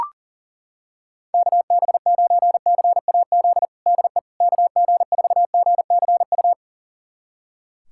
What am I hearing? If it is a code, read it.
Morse code, heard as a keyed tone, about 40 wpm, 700 Hz. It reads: KB9CAZ DE KG4GCU
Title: USA Callsign DE Callsign 40wpm